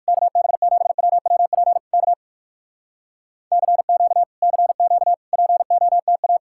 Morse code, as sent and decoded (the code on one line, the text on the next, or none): -.- -... --... .-- .-- .--. / -.- / -.-. --.- / -.-. --.- / .--. --- - .-
KB7WWP K CQ CQ POTA